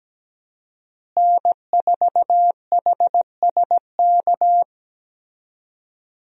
Morse code, heard as a keyed tone, about 17 wpm, 700 Hz. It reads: N4HSK
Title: USA Call Signs - 5 Characters - Repeated 3x - 17wpm